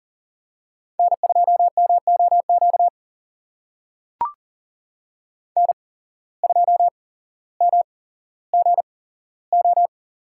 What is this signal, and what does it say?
Morse code, heard as a keyed tone, about 40 wpm, spaced as for 17 wpm, 700 Hz. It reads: N2MOQ N2MGO
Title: USA Call Signs - 5 Characters - 8x Character Spacing - 40wpm